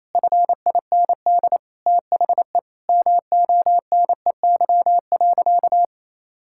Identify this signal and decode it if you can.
Morse code, heard as a keyed tone, about 27 wpm, 700 Hz. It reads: FIND THE MONEY.
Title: Sentences from Top 1600 Words 28wpm